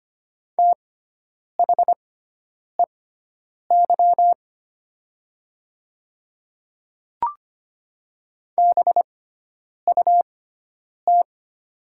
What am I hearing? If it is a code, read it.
Morse code, heard as a keyed tone, about 25 wpm, spaced as for 9 wpm, 700 Hz. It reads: THEY BUT
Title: 6x Character Spacing - Top 100 Words - 25wpm